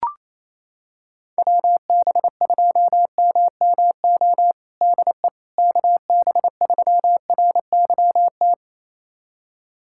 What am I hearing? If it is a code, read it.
Morse code, heard as a keyed tone, about 27 wpm, 700 Hz. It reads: WB2MMO DE KB3RYT
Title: USA Callsign DE Callsign - 6 Characters - 28wpm